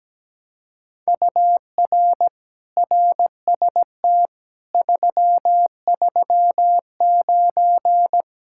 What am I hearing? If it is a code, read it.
Morse code, heard as a keyed tone, about 17 wpm, 700 Hz. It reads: UR RST 339
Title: Speed-Racing Common QSO Phrases Encore - 17wpm